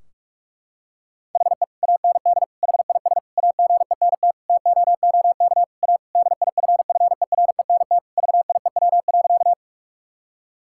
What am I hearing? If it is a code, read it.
Morse code, heard as a keyed tone, about 41 wpm, 700 Hz. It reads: HE AND HIS AGENT TOOK A DIFFERENT VIEW.
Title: Sentences from Top 1600 Words 45wpm